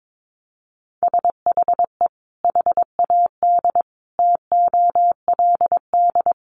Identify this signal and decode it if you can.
Morse code, heard as a keyed tone, about 22 wpm, 700 Hz. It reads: SHE HAD TOLD
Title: Sentences from Top 1500 Words 22wpm